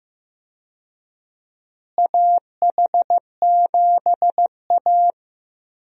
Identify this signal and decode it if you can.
Morse code, heard as a keyed tone, about 15 wpm, 700 Hz. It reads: AH7A
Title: USA Call Signs - 4 Characters - Repeated 2x - 15wpm